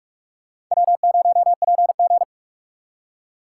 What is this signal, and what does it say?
Morse code, heard as a keyed tone, about 46 wpm, 700 Hz. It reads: W0PG
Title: USA Call Signs - 4 Characters - 4x Character Spacing - 45wpm